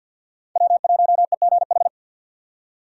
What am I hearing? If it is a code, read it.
Morse code, heard as a keyed tone, about 53 wpm, 700 Hz. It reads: W1EGH
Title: USA Call Signs - 5 Characters - 4x Character Spacing - 50wpm